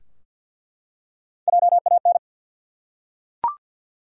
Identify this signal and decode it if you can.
Morse code, heard as a keyed tone, about 46 wpm, 700 Hz. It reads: JAN
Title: USA Common Names - Max 3 Letters - 50wpm